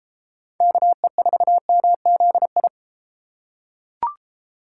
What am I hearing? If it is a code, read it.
Morse code, heard as a keyed tone, about 33 wpm, 700 Hz. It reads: KE4MZI